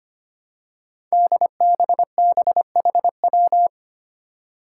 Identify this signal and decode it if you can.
Morse code, heard as a keyed tone, about 25 wpm, 700 Hz. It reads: DBBHW
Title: Five Challenging Letter-Number Combinations 25wpm